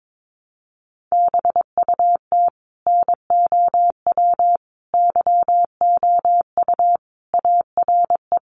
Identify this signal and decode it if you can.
Morse code, heard as a keyed tone, about 22 wpm, 700 Hz. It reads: BUT NOW YOU ARE
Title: Sentences from Top 1000 Words 22wpm